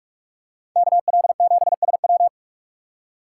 Morse code, heard as a keyed tone, about 45 wpm, 700 Hz. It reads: KL7SW